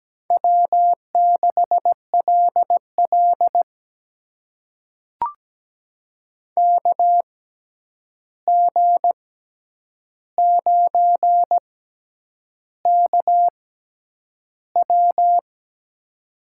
Morse code, heard as a keyed tone, about 17 wpm, spaced as for 8 wpm, 700 Hz. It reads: W6LL KG9KW
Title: USA Call Signs - 6x Character Spacing - 17wpm